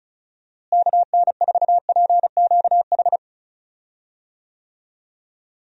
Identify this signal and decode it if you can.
Morse code, heard as a keyed tone, about 36 wpm, 700 Hz. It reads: KN4PQH
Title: USA Call Signs - 6 Characters - Repeated 2x - 35wpm